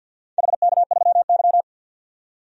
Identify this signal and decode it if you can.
Morse code, heard as a keyed tone, about 52 wpm, 700 Hz. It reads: HK3Y
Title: DXCC-100 Call Signs - 4 Characters - 50wpm